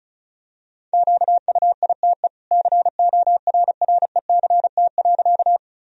Morse code, heard as a keyed tone, about 34 wpm, 700 Hz. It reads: QUITE CORRECT.
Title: Sentences from Top 1000 Words - Review Part 2 - 35wpm